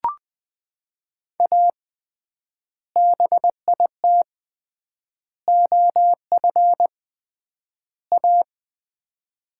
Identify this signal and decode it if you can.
Morse code, heard as a keyed tone, about 21 wpm, 700 Hz. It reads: A BIT OF A
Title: 3x Word-Spacing - Sets of 4 Words - 20wpm